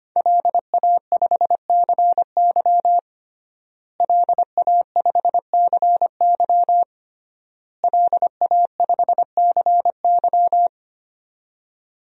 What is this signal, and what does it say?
Morse code, heard as a keyed tone, about 25 wpm, 700 Hz. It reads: LA5CY LA5CY LA5CY
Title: DXCC-100 Call Signs - 5 Characters - Repeated 3x - 25wpm